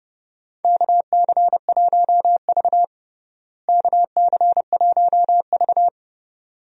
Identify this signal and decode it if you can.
Morse code, heard as a keyed tone, about 30 wpm, 700 Hz. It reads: KC1V KC1V
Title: USA Call Signs - 4 Characters - Repeated 2x - 30wpm